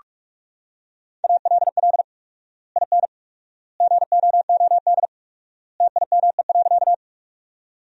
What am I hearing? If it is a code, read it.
Morse code, heard as a keyed tone, about 41 wpm, 700 Hz. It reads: ALL IN GOOD TIME.
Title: 4x Word-Spacing - Sentences from Top 200 Words - 45wpm